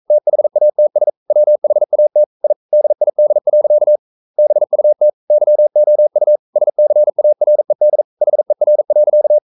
Morse code, heard as a keyed tone, about 42 wpm, 600 Hz. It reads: THATS WHAT I DID. BUT YOU SCARED HER.
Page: Romanen Never af Ken Follet på engelsk.